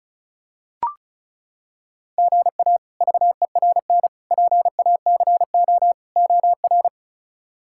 Morse code, heard as a keyed tone, about 34 wpm, 700 Hz. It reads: GA VERN PACO OR
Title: K1USN SST QSO Practice - 35wpm